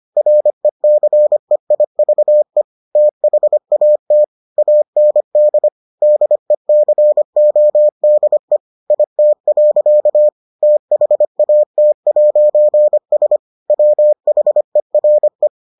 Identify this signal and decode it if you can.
Morse code, heard as a keyed tone, about 25 wpm, 600 Hz. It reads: RECEIVE THAT AND DECODE IT. THAT'S WHERE